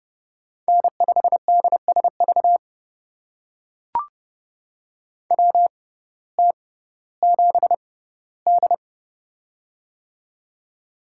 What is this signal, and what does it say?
Morse code, heard as a keyed tone, about 29 wpm, spaced as for 16 wpm, 700 Hz. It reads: N5DSV WT7D
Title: USA Call Signs - 6x Character Spacing - 30wpm